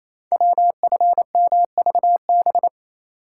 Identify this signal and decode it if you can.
Morse code, heard as a keyed tone, about 27 wpm, 700 Hz. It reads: WFMVB
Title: Code Groups - 5 Letters - 28wpm